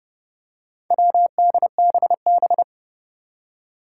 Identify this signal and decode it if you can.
Morse code, heard as a keyed tone, about 30 wpm, 700 Hz. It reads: WDBB
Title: Four Challenging Letter-Number Combinations 30wpm